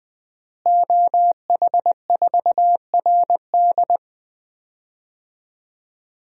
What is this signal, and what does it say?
Morse code, heard as a keyed tone, about 20 wpm, 700 Hz. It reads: OH4RD
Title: DXCC-100 Call Signs - 5 Characters - Repeated 3x - 20wpm